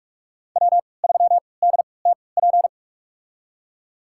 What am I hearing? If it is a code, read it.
Morse code, heard as a keyed tone, about 42 wpm, spaced as for 25 wpm, 700 Hz. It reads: W3DTP
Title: USA Call Signs - 5 Characters - 3x Character Spacing - 45wpm